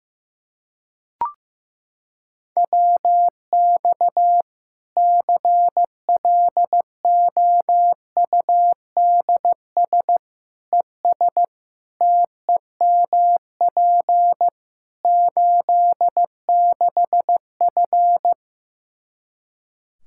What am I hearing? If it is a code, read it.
Morse code, heard as a keyed tone, about 15 wpm, 700 Hz. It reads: WX CLOUDS ES TEMP 86F
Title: Practice QSO - Exchange 2 - Info - 15wpm